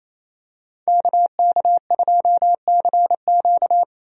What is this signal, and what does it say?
Morse code, heard as a keyed tone, about 28 wpm, 700 Hz. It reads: KK2CQ
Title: USA Call Signs - 5 Characters - 28wpm